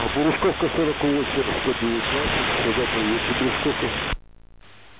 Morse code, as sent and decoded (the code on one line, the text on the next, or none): none